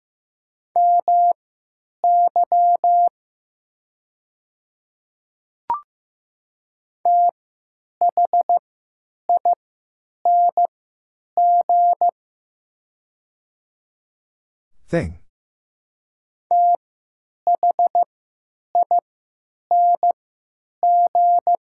Morse code, heard as a keyed tone, about 15 wpm, spaced as for 8 wpm, 700 Hz. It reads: MY THING THING
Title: 3x Character Spacing - Top 100 Words - 15wpm